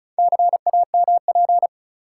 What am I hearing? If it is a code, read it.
Morse code, heard as a keyed tone, about 36 wpm, 700 Hz. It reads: CAMP
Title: Four Letter Words - 35wpm